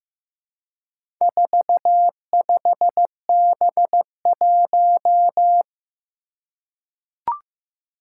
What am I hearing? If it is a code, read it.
Morse code, heard as a keyed tone, about 15 wpm, 700 Hz. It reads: 45B1
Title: Four Challenging Letter-Number Combinations 15wpm